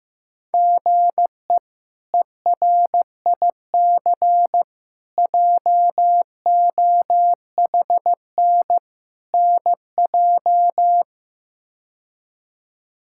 Morse code, heard as a keyed tone, about 16 wpm, 700 Hz. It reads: GE ERIC JOHN NJ
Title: K1USN SST QSO Practice - 15wpm